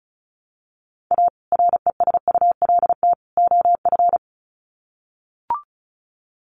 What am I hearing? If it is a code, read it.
Morse code, heard as a keyed tone, about 34 wpm, 700 Hz. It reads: A RESULT OF
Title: Sets of 3 Words 35wpm